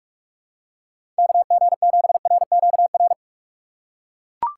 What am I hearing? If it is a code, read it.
Morse code, heard as a keyed tone, about 42 wpm, 700 Hz. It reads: KG7RQR
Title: USA Call Signs - 6 Characters - 6x Character Spacing - 45wpm